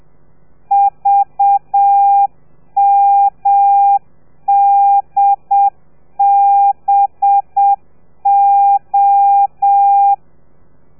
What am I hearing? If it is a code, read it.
Morse code, heard as a keyed tone, about 7 wpm, 800 Hz. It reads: VMDBO
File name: telegrafia / i2tfj / Lez4 30